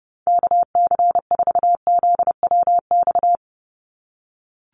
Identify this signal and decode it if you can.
Morse code, heard as a keyed tone, about 30 wpm, 700 Hz. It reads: KC4ZWX
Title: USA Call Signs 30wpm